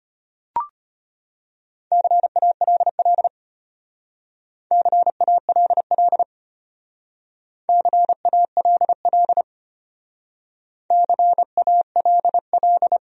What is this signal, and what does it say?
Morse code, heard as a keyed tone, about 31 wpm, 700 Hz. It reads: CALL CALL CALL CALL
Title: Speed-Racing Top 200 Words 25wpm